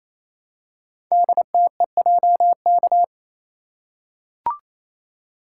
Morse code, heard as a keyed tone, about 28 wpm, 700 Hz. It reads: DTEJK
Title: Code Groups - 5 Letters - 28wpm